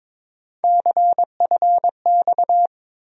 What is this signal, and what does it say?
Morse code, heard as a keyed tone, about 22 wpm, 700 Hz. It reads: CFX
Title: USA Call Sign Postfixes - 3 Letters 22wpm